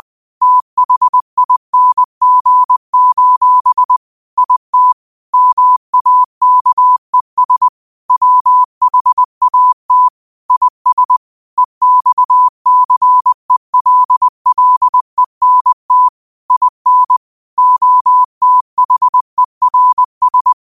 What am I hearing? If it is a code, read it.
Morse code, heard as a keyed tone, about 20 wpm, 1000 Hz. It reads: THING: IT MAKES WHAT IS EXCELLENT IN OTHERS
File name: qotdfiles / 20wpm